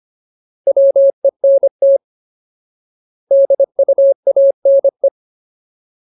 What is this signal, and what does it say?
Morse code, heard as a keyed tone, about 25 wpm, 550 Hz. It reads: WENT DUANE
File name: Ultimate Word List / 25WPM Ultimate